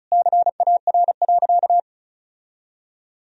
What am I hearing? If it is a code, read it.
Morse code, heard as a keyed tone, about 34 wpm, 700 Hz. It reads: CAR.
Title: Sentences from Top 1000 Words - Review Part 1 - 35wpm